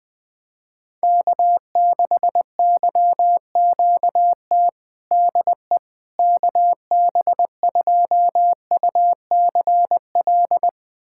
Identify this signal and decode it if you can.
Morse code, heard as a keyed tone, about 21 wpm, 700 Hz. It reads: K6YQT DE KB2UCL
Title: USA Callsign DE Callsign 20wpm